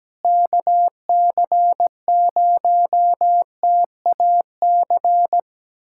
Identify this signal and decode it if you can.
Morse code, heard as a keyed tone, about 17 wpm, 700 Hz. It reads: KC0TAC
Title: USA Callsign DE Callsign - 6 Characters - 17wpm